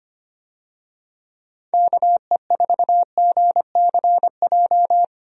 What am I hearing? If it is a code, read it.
Morse code, heard as a keyed tone, about 25 wpm, 700 Hz. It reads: KE4GCJ
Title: USA Call Signs - 6 Characters - Repeated 3x - 25wpm